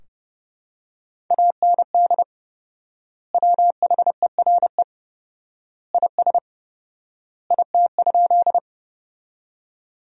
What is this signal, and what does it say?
Morse code, heard as a keyed tone, about 32 wpm, 700 Hz. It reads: AND WHERE IS IT?
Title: 4x Word-Spacing - Sentences from Top 200 Words - 30wpm